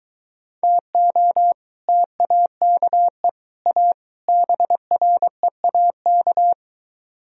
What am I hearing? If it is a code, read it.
Morse code, heard as a keyed tone, about 23 wpm, 700 Hz. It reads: TO TAKE A BREAK